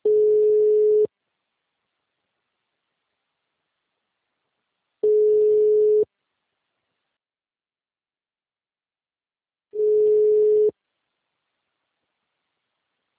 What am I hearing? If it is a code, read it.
Ringback tone (Europe, 425 Hz).